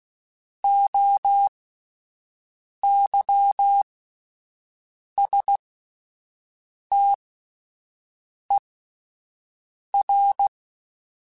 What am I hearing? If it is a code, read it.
Morse code, heard as a keyed tone, about 16 wpm, spaced as for 5 wpm, 800 Hz. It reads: OYSTER
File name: Alphabetical / Alphabetical mp3s / Os @5